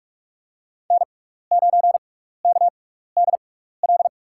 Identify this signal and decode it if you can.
Morse code, heard as a keyed tone, about 42 wpm, spaced as for 15 wpm, 700 Hz. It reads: N9KDL
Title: USA Call Signs - 5 Characters - 6x Character Spacing - 45wpm